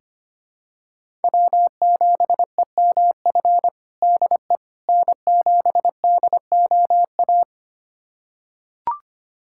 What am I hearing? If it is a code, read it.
Morse code, heard as a keyed tone, about 25 wpm, 700 Hz. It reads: W7EMF DE N7DOA
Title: USA Callsign DE Callsign - 5 Characters - 25wpm